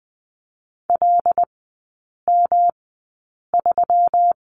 Morse code, heard as a keyed tone, about 20 wpm, spaced as for 8 wpm, 700 Hz. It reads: LM3
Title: 198 Lesson 32e - Letter k - Three Characters 20wpm